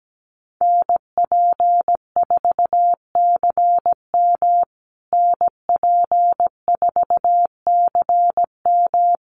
Morse code, heard as a keyed tone, about 17 wpm, 700 Hz. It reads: NP4CM NP4CM
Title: USA Call Signs Repeated 3x 17wpm